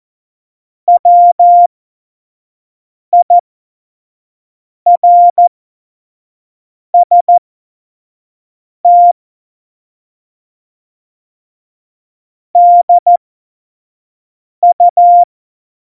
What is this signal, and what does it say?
Morse code, heard as a keyed tone, about 14 wpm, spaced as for 5 wpm, 700 Hz. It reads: WIRST DU
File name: Morsekurs / Witze / Witz 11. Dezember 2018, 25 F70